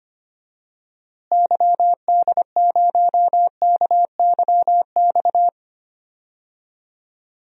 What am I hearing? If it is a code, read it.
Morse code, heard as a keyed tone, about 25 wpm, 700 Hz. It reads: YD0KYX
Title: DXCC-100 Call Signs - 6 Characters - Repeated 2x - 25wpm